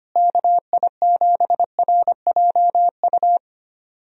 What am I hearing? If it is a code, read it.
Morse code, heard as a keyed tone, about 25 wpm, 700 Hz. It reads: KI7RJU
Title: USA Call Signs - 6 Characters - Repeated 3x - 25wpm